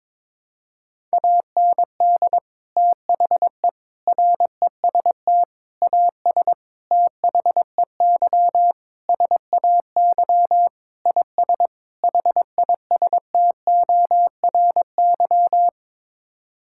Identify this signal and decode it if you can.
Morse code, heard as a keyed tone, about 22 wpm, 700 Hz. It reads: AND THE REST AS THEY SAY IS HISTORY
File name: practice / 22wpm / Sets of 8 Words - 22wpm